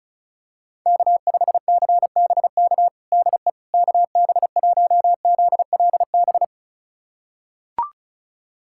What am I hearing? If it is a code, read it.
Morse code, heard as a keyed tone, about 36 wpm, 700 Hz. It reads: K5CBK DE KB1ZLB
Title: USA Callsign DE Callsign 35wpm